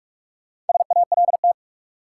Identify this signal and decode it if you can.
Morse code, heard as a keyed tone, about 46 wpm, 700 Hz. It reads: SALT
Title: Sets of 5 Words - New Word Drill - 45wpm